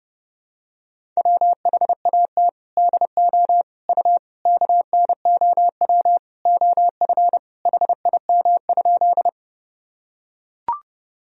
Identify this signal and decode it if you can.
Morse code, heard as a keyed tone, about 30 wpm, 700 Hz. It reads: WHAT DO U KNOW OF HIM?
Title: Sentences from Top 100 Words and Abbreviations 30wpm